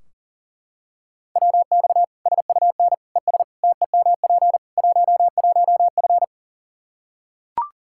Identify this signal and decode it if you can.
Morse code, heard as a keyed tone, about 40 wpm, 700 Hz. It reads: WX SUN ES TEMP 11F
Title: Practice QSO - Exchange 2 - Info - 40wpm